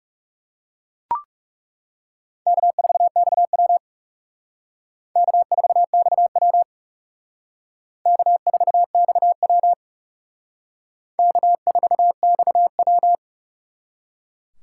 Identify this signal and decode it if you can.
Morse code, heard as a keyed tone, about 37 wpm, 700 Hz. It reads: K4XW K4XW K4XW K4XW
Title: Speed Racing - USA 1x2 Call Signs 30wpm